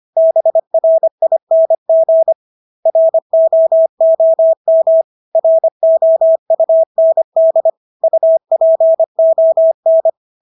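Morse code, heard as a keyed tone, about 25 wpm, 650 Hz. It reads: BRING ROOM ROUND UPON
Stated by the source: Morse practice of English words at 25 wpm from CW med Gnister.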